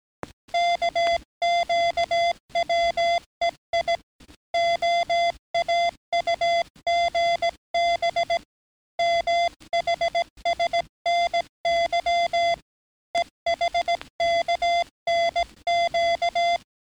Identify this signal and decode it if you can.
Morse code, heard as a keyed tone, about 17 wpm, 690 Hz. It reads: KQWEI OAUGB MHSNY EHKNQ
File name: training / NAVPERS 11049to11060 / NAVPERS 11057 RA RB